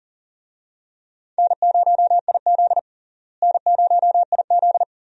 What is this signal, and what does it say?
Morse code, heard as a keyed tone, about 40 wpm, 700 Hz. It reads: N0IZ N0IZ
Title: USA Call Signs - 4 Characters - Repeated 3x - 40wpm